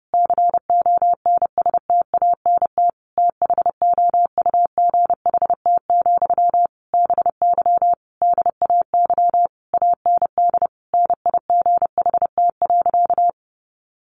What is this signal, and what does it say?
Morse code, heard as a keyed tone, about 30 wpm, 700 Hz. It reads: CONSTANT THOUGHT, BY DAY AND NIGHT.
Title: Sentences from Top 1500 Words 30wpm